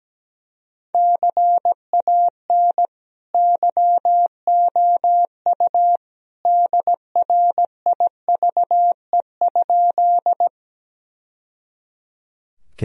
Morse code, heard as a keyed tone, about 17 wpm, 700 Hz. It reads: CAN YOU DRIVE?
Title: Sentences from Top 1000 Words - Review Part 2 - 17wpm